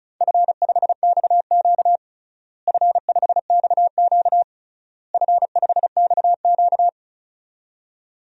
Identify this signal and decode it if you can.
Morse code, heard as a keyed tone, about 34 wpm, 700 Hz. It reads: F5XQ F5XQ F5XQ
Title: DXCC-100 Call Signs - 4 Characters - Repeated 3x - 35wpm